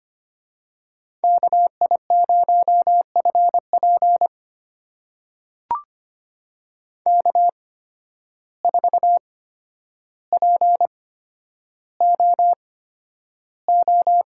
Morse code, heard as a keyed tone, about 25 wpm, spaced as for 11 wpm, 700 Hz. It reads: KI0FP K4POO
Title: USA Call Signs - 5 Characters - 8x Character Spacing - 25wpm